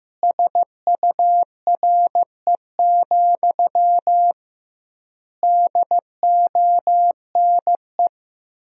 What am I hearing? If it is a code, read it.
Morse code, heard as a keyed tone, about 15 wpm, 700 Hz. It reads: SURE, DONE
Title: 2x Word-Spacing - Sentences from Top 500 Words - 15wpm